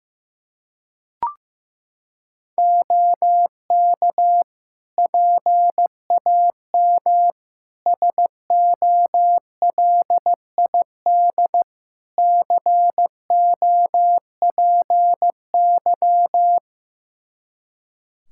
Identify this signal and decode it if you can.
Morse code, heard as a keyed tone, about 15 wpm, 700 Hz. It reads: OK PAM SOLID COPY
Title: Practice QSO - Exchange 3 - More Info - 15wpm